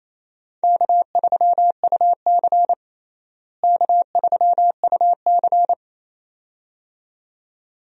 Morse code, heard as a keyed tone, about 28 wpm, 700 Hz. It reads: K3UC K3UC
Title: USA Call Signs - 4 Characters - Repeated 2x - 28wpm